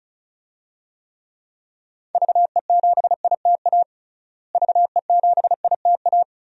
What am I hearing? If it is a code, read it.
Morse code, heard as a keyed tone, about 36 wpm, 700 Hz. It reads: VE7ITA VE7ITA
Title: DXCC-100 Call Signs - 6 Characters - Repeated 2x - 35wpm